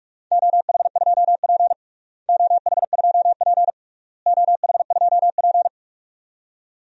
Morse code, heard as a keyed tone, about 42 wpm, 700 Hz. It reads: OH2P OH2P OH2P
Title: DXCC-100 Call Signs - 4 Characters - Repeated 3x - 45wpm